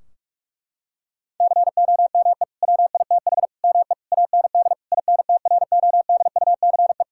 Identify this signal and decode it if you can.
Morse code, heard as a keyed tone, about 46 wpm, 700 Hz. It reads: COME WITH ME AND INTRODUCE
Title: Sentences from Top 1600 Words 45wpm